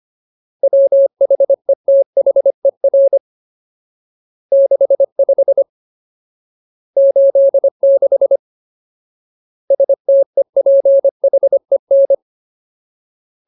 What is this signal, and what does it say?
Morse code, heard as a keyed tone, about 25 wpm, 550 Hz. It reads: WHETHER 65 86 STEPHEN
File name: Ultimate Word List / 25WPM Ultimate